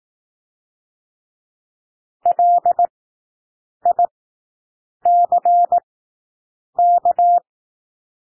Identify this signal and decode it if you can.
Morse code, heard as a keyed tone, about 18 wpm, spaced as for 7 wpm, 700 Hz. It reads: LICK
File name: SOUND WORDS 18 wpm